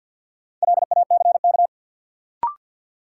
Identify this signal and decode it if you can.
Morse code, heard as a keyed tone, about 53 wpm, 700 Hz. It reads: LACK
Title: Sentences from Top 1100 Words - New Word Drill Alt - 50wpm